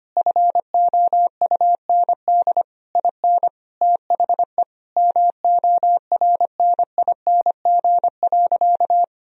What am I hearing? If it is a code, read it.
Morse code, heard as a keyed tone, about 25 wpm, 700 Hz. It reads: FOUND IN THE MORNING.
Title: Sentences from Top 1000 Words - Review Part 1 - 25wpm